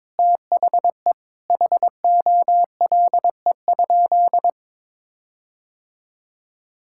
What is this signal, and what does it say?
Morse code, heard as a keyed tone, about 22 wpm, 700 Hz. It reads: THE HOLE?
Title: Sentences from Top 1600 Words 22wpm